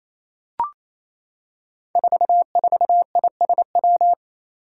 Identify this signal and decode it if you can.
Morse code, heard as a keyed tone, about 28 wpm, 700 Hz. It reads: 44ISW
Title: Five Challenging Letter-Number Combinations 28wpm